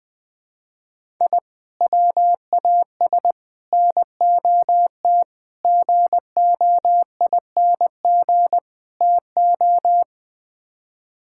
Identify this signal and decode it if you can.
Morse code, heard as a keyed tone, about 20 wpm, 700 Hz. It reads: I WAS NOT GOING TO
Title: Build A Phrase - 5-Word Phrases - 20wpm